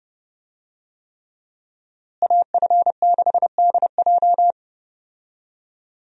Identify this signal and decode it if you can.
Morse code, heard as a keyed tone, about 31 wpm, 700 Hz. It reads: AF6DJ
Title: USA Call Signs - 5 Characters - Repeated 3x - 30wpm